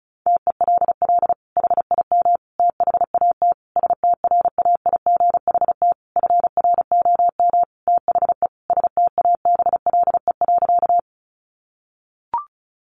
Morse code, heard as a keyed tone, about 34 wpm, 700 Hz. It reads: TELL HIM THAT STRAIGHT FROM THE STABLE.
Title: Sentences from Top 1500 Words 35wpm